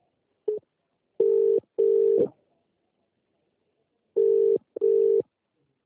Ringback tone (UK), 400 + 450 Hz.